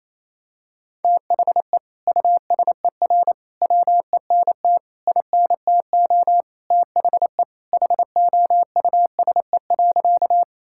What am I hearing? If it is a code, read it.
Morse code, heard as a keyed tone, about 28 wpm, 700 Hz. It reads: THE USER WENT INTO THE HOUSE.
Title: Sentences from Top 1000 Words - Review Part 5 - 28wpm